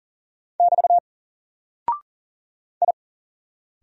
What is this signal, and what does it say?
Morse code, heard as a keyed tone, about 40 wpm, 700 Hz. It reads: <BT> I
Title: Single Character (Letter:Number:Punctuation:Prosign) 40wpm